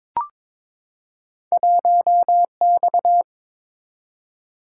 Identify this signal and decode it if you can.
Morse code, heard as a keyed tone, about 22 wpm, 700 Hz. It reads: 1X
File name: practice / 22wpm / Two Letter-Number Combinations - 22wpm